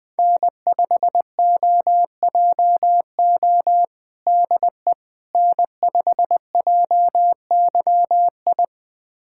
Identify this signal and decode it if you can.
Morse code, heard as a keyed tone, about 21 wpm, 700 Hz. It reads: N5OJO DE N5JYI
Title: USA Callsign DE Callsign - 5 Characters - 20wpm